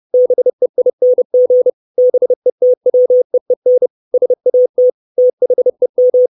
Morse code, heard as a keyed tone, about 30 wpm, 500 Hz. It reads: BEING BETWEEN SAT THEM
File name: 632 Words 30WPM